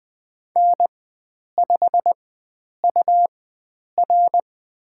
Morse code, heard as a keyed tone, about 20 wpm, spaced as for 9 wpm, 700 Hz. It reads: N5UR
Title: USA Call Signs - 4 Characters - 4x Character Spacing - 20wpm